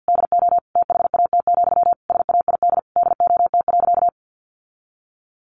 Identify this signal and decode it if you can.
Morse code, heard as a keyed tone, about 50 wpm, 700 Hz. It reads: DO THAT, SAID DOT.
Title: Sentences from Top 1500 Words 50wpm